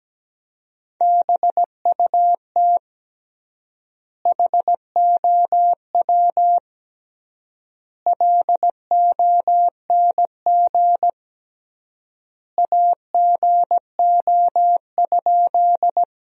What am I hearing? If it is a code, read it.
Morse code, heard as a keyed tone, about 17 wpm, 700 Hz. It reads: BUT HOW LONG AGO?
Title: 3x Word-Spacing - Sentences from Top 500 Words - 17wpm